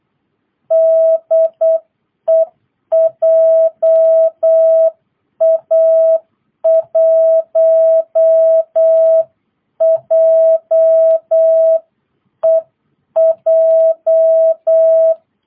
Morse code, heard as a keyed tone, about 8 wpm, 660 Hz. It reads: DE JA1JEJ